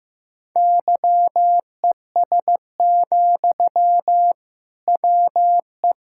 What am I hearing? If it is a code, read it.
Morse code, heard as a keyed tone, about 15 wpm, 690 Hz. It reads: YES, WE
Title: Sentences from Top 1000 Words - Review Part 5 - 15wpm